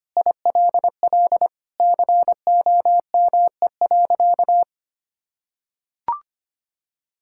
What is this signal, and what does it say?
Morse code, heard as a keyed tone, about 25 wpm, 700 Hz. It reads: ILL COME.
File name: practice / 25wpm / Sentences with 2-Word Phrases - 25wpm